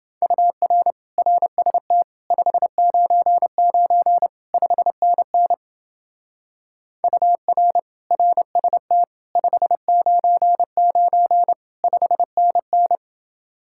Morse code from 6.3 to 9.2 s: UR RST

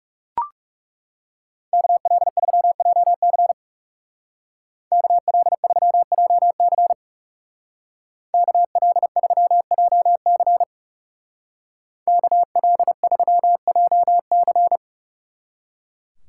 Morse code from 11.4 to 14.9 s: KL3JC

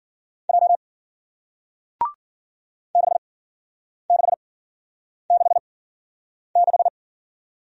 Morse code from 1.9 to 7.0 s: 6666

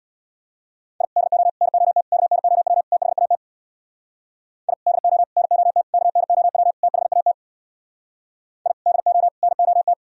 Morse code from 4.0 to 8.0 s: I DO NOT KNOW THAT

